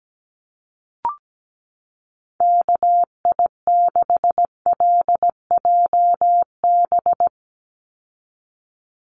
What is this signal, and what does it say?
Morse code, heard as a keyed tone, about 17 wpm, 700 Hz. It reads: KI6LJB
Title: USA Call Signs 17wpm